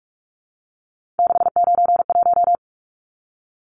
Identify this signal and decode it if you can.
Morse code, heard as a keyed tone, about 44 wpm, 700 Hz. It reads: 691
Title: Numbers - 3-Digit 45wpm